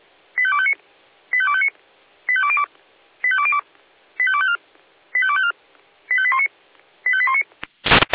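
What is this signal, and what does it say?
Five-tone selective call (ZVEI), heard as quick runs of five tones, tones of about 70 ms. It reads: 85259 85259 85202 85202 85254 85254 85819 85819
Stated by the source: A ZVEI five-tone selective call received by SDR.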